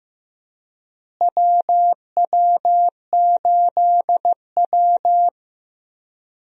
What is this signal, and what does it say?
Morse code, heard as a keyed tone, about 15 wpm, 700 Hz. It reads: WW8W
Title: USA Call Signs - 4 Characters - Repeated 3x - 15wpm